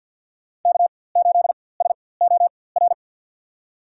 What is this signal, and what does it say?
Morse code, heard as a keyed tone, about 53 wpm, spaced as for 24 wpm, 700 Hz. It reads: K8SOR
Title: USA Call Signs - 5 Characters - 4x Character Spacing - 50wpm